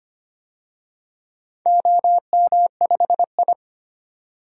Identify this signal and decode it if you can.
Morse code, heard as a keyed tone, about 25 wpm, 700 Hz. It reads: OM5I
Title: DXCC-100 Call Signs - 4 Characters - 25wpm